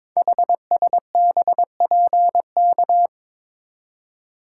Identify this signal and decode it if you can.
Morse code, heard as a keyed tone, about 22 wpm, 700 Hz. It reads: HSBPK
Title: Code Groups - 5 Letters - 22wpm